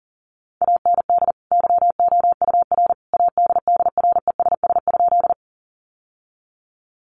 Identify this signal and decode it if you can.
Morse code, heard as a keyed tone, about 40 wpm, 700 Hz. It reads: AND YOUR ADDRESS?
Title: Sentences from Top 1000 Words 40wpm